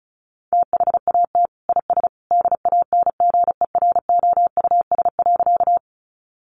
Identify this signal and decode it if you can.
Morse code, heard as a keyed tone, about 34 wpm, 700 Hz. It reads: THAT IS DANGEROUS.
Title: Sentences from Top 1200 Words 35wpm